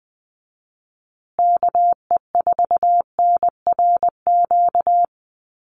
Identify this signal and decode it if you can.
Morse code, heard as a keyed tone, about 20 wpm, 700 Hz. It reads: KE4NRQ
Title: USA Call Signs Repeated 2x 20wpm